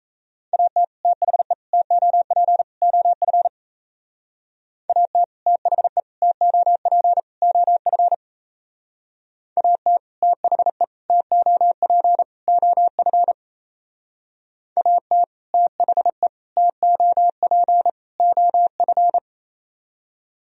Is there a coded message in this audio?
Yes: Morse code from 0.5 to 19.2 s, heard as a keyed tone, about 35 wpm, 700 Hz. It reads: AT THE TOP OF AT THE TOP OF AT THE TOP OF AT THE TOP OF